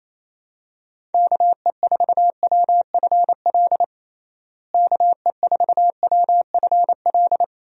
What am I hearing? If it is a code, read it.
Morse code, heard as a keyed tone, about 28 wpm, 700 Hz. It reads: KE4WFL KE4WFL
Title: USA Call Signs - 6 Characters - Repeated 3x - 28wpm